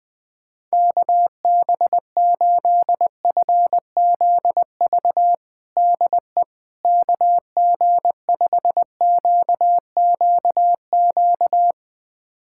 Morse code, heard as a keyed tone, about 20 wpm, 700 Hz. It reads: KB8FZV DE KG5QQQ
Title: USA Callsign DE Callsign - 6 Characters - 20wpm